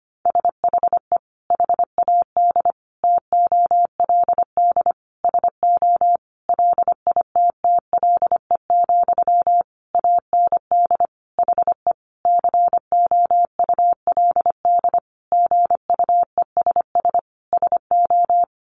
Morse code, heard as a keyed tone, about 25 wpm, 700 Hz. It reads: SHE HAD TOLD SO LITTLE, AND HE COULD GUESS SO
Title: Sentences from Top 1500 Words 25wpm